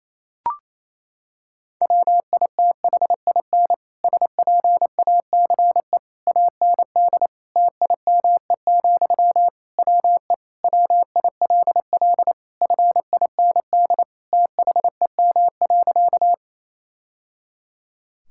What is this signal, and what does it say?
Morse code, heard as a keyed tone, about 28 wpm, 700 Hz. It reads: WITHIN SPACE AND TIME, WE WILL FIND THEM.
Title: Sentences from Top 1000 Words - Review Part 5 - 28wpm